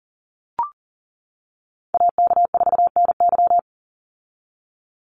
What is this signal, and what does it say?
Morse code, heard as a keyed tone, about 38 wpm, 700 Hz. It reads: AK4NY
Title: Call Signs - 40wpm